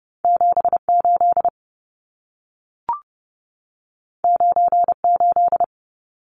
Morse code, heard as a keyed tone, about 30 wpm, 700 Hz. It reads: 78 98
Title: Numbers - 2-Digit 30wpm